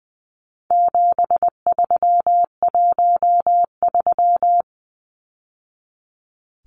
Morse code, heard as a keyed tone, about 20 wpm, 700 Hz. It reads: 7313
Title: Numbers - 4-Digit 20wpm